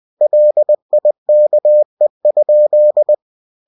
Morse code, heard as a keyed tone, about 20 wpm, 600 Hz. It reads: LIKE?
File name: cw-books / winnie-the-pooh / winnie-20wpm-page-0006